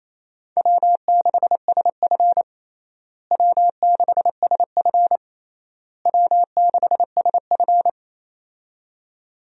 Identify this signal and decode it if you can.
Morse code, heard as a keyed tone, about 28 wpm, 700 Hz. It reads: W6SF W6SF W6SF
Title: USA Call Signs - 4 Characters - Repeated 3x - 28wpm